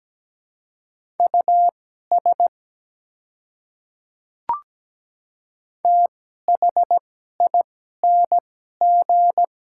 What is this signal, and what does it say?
Morse code, heard as a keyed tone, about 17 wpm, spaced as for 12 wpm, 700 Hz. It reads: US THING
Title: 2x Character Spacing - Top 200 Words - 17wpm